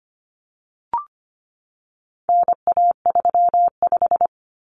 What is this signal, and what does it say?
Morse code, heard as a keyed tone, about 25 wpm, 700 Hz. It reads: NA35
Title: Common Cut Numbers - 4-Digit 25wpm